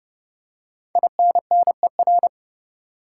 Morse code, heard as a keyed tone, about 30 wpm, 700 Hz. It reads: INNER
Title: Five Letter Words - 30wpm